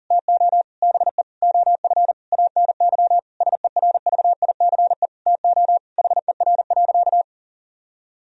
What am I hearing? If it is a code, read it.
Morse code, heard as a keyed tone, about 40 wpm, 700 Hz. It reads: TO BE OF ANY SERVICE TO HER.
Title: Sentences from Top 1000 Words - Review Part 4 - 40wpm